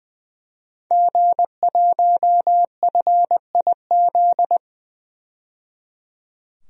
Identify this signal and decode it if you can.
Morse code, heard as a keyed tone, about 20 wpm, 700 Hz. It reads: G1FIZ
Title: DXCC-100 Call Signs - 5 Characters - 20wpm